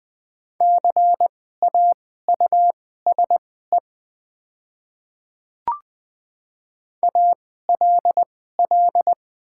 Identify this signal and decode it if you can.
Morse code, heard as a keyed tone, about 20 wpm, spaced as for 15 wpm, 690 Hz. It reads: CAUSE ALL
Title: 2x Character Spacing - Top 200 Words - 20wpm